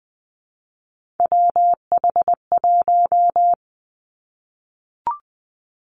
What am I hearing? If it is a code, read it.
Morse code, heard as a keyed tone, about 20 wpm, 700 Hz. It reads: WH1
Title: Three Challenging Letter-Number Combinations 20wpm